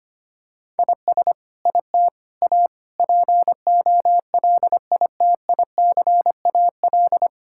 Morse code, heard as a keyed tone, about 25 wpm, 700 Hz. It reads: IS IT A POLITICAL
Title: Sentences from Top 1000 Words - Review Part 5 - 25wpm